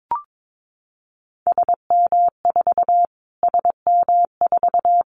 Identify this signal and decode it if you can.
Morse code, heard as a keyed tone, about 22 wpm, 700 Hz. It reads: SM4 SM4
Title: Callsign Prefixes for Top 25 DXCC - Repeated 3x - 22wpm